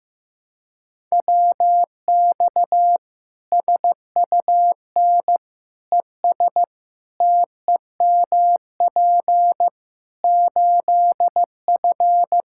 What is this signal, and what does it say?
Morse code, heard as a keyed tone, about 15 wpm, 700 Hz. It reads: WX SUN ES TEMP 8F
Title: Practice QSO - Exchange 2 - Info - 15wpm